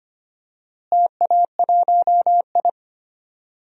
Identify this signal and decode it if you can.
Morse code, heard as a keyed tone, about 25 wpm, 700 Hz. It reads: TA1I
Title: DXCC-100 Call Signs - 4 Characters - Repeated 3x - 25wpm